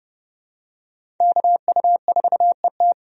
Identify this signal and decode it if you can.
Morse code, heard as a keyed tone, about 30 wpm, 700 Hz. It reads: KU4ET
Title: USA Call Signs - 5 Characters - 30wpm